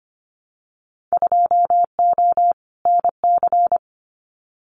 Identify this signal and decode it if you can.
Morse code, heard as a keyed tone, about 25 wpm, 700 Hz. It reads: 2O NC
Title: Winter Field Day Exchanges 25wpm